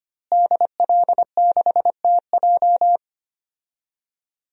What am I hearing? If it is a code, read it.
Morse code, heard as a keyed tone, about 25 wpm, 700 Hz. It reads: DL6TJ
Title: DXCC-100 Call Signs - 5 Characters - Repeated 3x - 25wpm